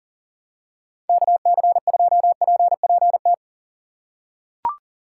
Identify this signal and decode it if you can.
Morse code, heard as a keyed tone, about 37 wpm, 700 Hz. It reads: KC2PPT
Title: USA Call Signs - 6 Characters - 6x Character Spacing - 40wpm